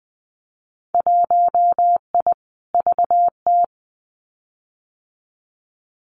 Morse code, heard as a keyed tone, about 20 wpm, 700 Hz. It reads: 1I VT
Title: Winter Field Day Exchanges Encore 20wpm